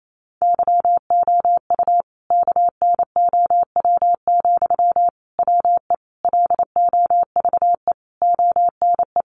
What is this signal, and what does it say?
Morse code, heard as a keyed tone, about 28 wpm, 700 Hz. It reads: YOU KNOW, WE LOVE ONE
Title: Sentences from Top 600 Words 28wpm